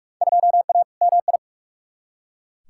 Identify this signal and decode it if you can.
Morse code, heard as a keyed tone, about 43 wpm, 700 Hz. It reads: 2A MI
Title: Field Day Exchanges - 45wpm